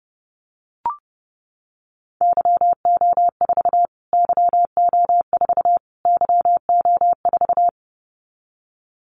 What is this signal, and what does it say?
Morse code, heard as a keyed tone, about 30 wpm, 700 Hz. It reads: YO4 YO4 YO4
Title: Callsign Prefixes for Top 25 DXCC - Repeated 3x - 30wpm